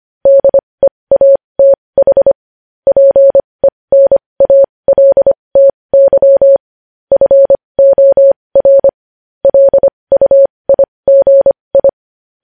Morse code, heard as a keyed tone, about 25 wpm, spaced as for 20 wpm, 550 Hz. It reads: DEATH PENALTY FOR LUIGI
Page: news headlines in morse code at 20 wpm